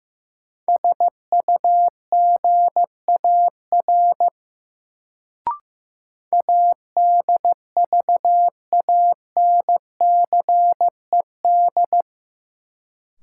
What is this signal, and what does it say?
Morse code, heard as a keyed tone, about 15 wpm, 700 Hz. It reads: SUGAR ADVANCED
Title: Sentences from Top 1200 Words - New Word Drill Alt - 15wpm